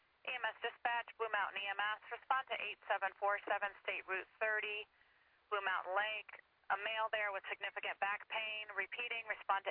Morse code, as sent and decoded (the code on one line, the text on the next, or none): none